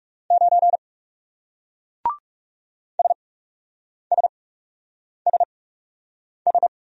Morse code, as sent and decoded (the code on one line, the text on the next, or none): ----. / ... ... ... ...
9 SSSS